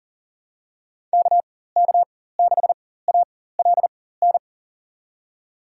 Morse code, heard as a keyed tone, about 40 wpm, spaced as for 19 wpm, 700 Hz. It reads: KK6ALN
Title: USA Call Signs - 6 Characters - 4x Character Spacing - 40wpm